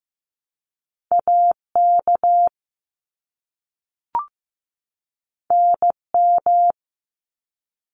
Morse code, heard as a keyed tone, about 15 wpm, 700 Hz. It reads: AK NM
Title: ARRL Sections 15wpm